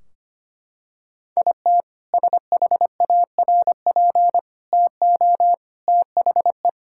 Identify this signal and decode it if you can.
Morse code, heard as a keyed tone, about 25 wpm, 700 Hz. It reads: IT SHARP TO THE
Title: Sentences from Top 1600 Words 25wpm